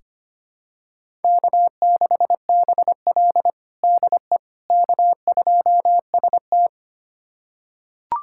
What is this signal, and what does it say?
Morse code, heard as a keyed tone, about 25 wpm, 700 Hz. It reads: K6BL DE K2ST
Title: USA Callsign DE Callsign - 4 Characters - 25wpm